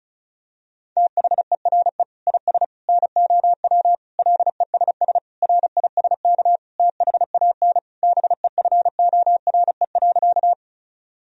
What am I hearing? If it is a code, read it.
Morse code, heard as a keyed tone, about 34 wpm, 700 Hz. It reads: THERE IS NOW LESS RISK THAN BEFORE.
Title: Sentences from Top 1000 Words - Review Part 4 - 35wpm